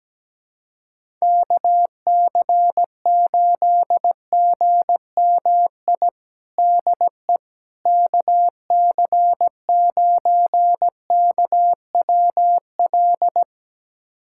Morse code, heard as a keyed tone, about 17 wpm, 700 Hz. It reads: KC8GMI DE KC9KWL
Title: USA Callsign DE Callsign - 6 Characters - 17wpm